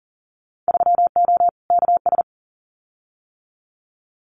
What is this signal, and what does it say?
Morse code, heard as a keyed tone, about 40 wpm, 700 Hz. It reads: 3O KS
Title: Winter Field Day Exchanges 40wpm